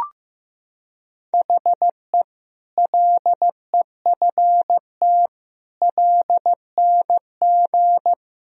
Morse code, heard as a keyed tone, about 16 wpm, 700 Hz. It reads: HE LEFT LNG
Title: Sentences from Top 500 Words and Abbreviations 15wpm